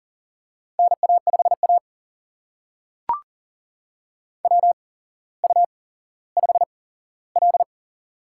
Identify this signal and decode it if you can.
Morse code, heard as a keyed tone, about 42 wpm, spaced as for 17 wpm, 700 Hz. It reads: NA5A WU5L
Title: USA Call Signs - 4 Characters - 8x Character Spacing - 40wpm